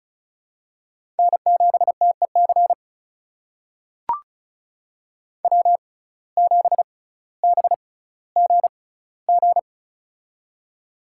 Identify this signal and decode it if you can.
Morse code, heard as a keyed tone, about 36 wpm, spaced as for 18 wpm, 700 Hz. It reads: N7TEC W7BGG
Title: USA Call Signs - 6x Character Spacing - 35wpm